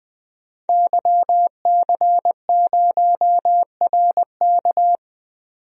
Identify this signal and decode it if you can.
Morse code, heard as a keyed tone, about 20 wpm, 700 Hz. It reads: YC0RK
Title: DXCC-100 Call Signs - 5 Characters - 20wpm